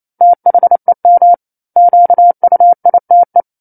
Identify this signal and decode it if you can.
Morse code, heard as a keyed tone, about 28 wpm, 700 Hz. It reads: THEM QUITE
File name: 09-Chapter-09 28 WPM